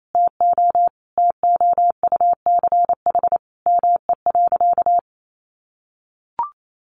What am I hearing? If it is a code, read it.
Morse code, heard as a keyed tone, about 28 wpm, 700 Hz. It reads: TO TOUCH ME.
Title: Sentences from Top 1200 Words 28wpm